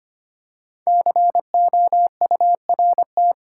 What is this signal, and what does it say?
Morse code, heard as a keyed tone, about 25 wpm, 700 Hz. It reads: COURT
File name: 5 Letter Words - First 3 Letters Given - 25wpm